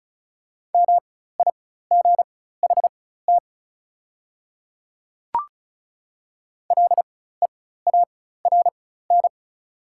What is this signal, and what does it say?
Morse code, heard as a keyed tone, about 36 wpm, spaced as for 16 wpm, 690 Hz. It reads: MIGHT LEARN
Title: 4x Character Spacing - Top 500 Words - 35wpm